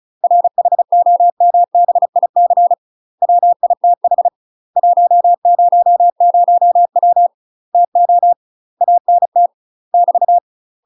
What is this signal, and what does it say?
Morse code, heard as a keyed tone, about 36 wpm, 700 Hz. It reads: RHOMBIC WITH 100W TO ANT <BT>
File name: UK MASTER QSOs TRAINING / 35WPM / UK MASTER QSOs 0018